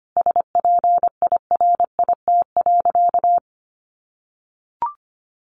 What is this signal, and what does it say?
Morse code, heard as a keyed tone, about 25 wpm, 700 Hz. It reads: SPIRIT.
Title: Sentences from Top 1400 Words 25wpm